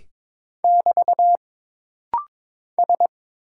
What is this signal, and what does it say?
Morse code, heard as a keyed tone, about 22 wpm, 700 Hz. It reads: <BT> S
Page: This rapid-fire timing is geared to push your ICR (Instant Character Recognition) skills.